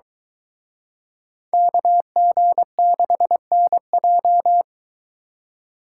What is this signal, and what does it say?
Morse code, heard as a keyed tone, about 22 wpm, 700 Hz. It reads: KG6NJ